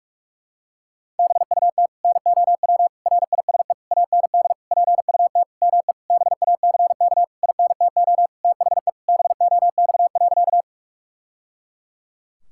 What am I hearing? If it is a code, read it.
Morse code, heard as a keyed tone, about 42 wpm, 700 Hz. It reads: BUT NOW RISE AND PUT ME BACK INTO THE BOX.
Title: Sentences from Top 1000 Words - Review Part 6 - 45wpm